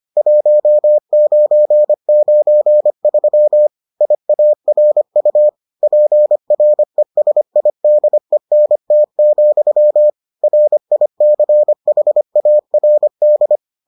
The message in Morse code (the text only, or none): 1993 IARU PRESIDENT, RICHARD